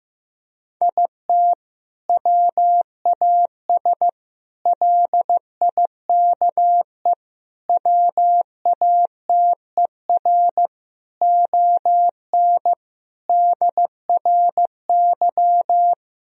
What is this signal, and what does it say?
Morse code, heard as a keyed tone, about 15 wpm, 700 Hz. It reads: IT WAS LIKE WATER ON DRY
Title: Sentences from Top 1000 Words - Review Part 4 - 15wpm